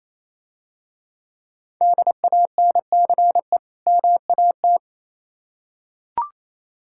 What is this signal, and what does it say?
Morse code, heard as a keyed tone, about 27 wpm, 690 Hz. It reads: DANCE MAT
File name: Easy Words - Speed-Racing with Hints - 22wpm